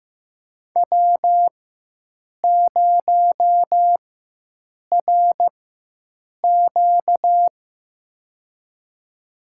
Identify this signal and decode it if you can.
Morse code, heard as a keyed tone, about 15 wpm, spaced as for 7 wpm, 700 Hz. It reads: W0RQ
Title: USA Call Signs - 4 Characters - 4x Character Spacing - 15wpm